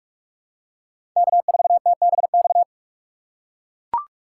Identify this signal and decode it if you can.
Morse code, heard as a keyed tone, about 43 wpm, 700 Hz. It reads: K4TBX
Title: USA Call Signs - 4x Character Spacing - 45wpm